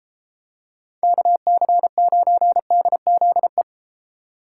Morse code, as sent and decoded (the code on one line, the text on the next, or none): -.- -.-. ----. -.. --.. .
KC9DZE